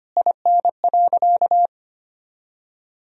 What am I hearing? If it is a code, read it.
Morse code, heard as a keyed tone, about 25 wpm, 700 Hz. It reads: IN.
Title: Sentences from Top 1000 Words - Review Part 2 - 25wpm